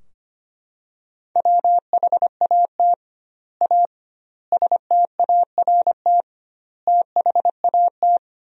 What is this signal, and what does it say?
Morse code, heard as a keyed tone, about 25 wpm, 700 Hz. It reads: WHAT A START THAT
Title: 2x Word-Spacing - Sentences from Top 300 Words - 25wpm